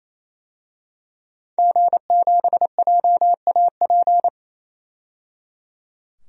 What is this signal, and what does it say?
Morse code, heard as a keyed tone, about 28 wpm, 700 Hz. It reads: G7JAP
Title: DXCC-100 Call Signs - 5 Characters - 28wpm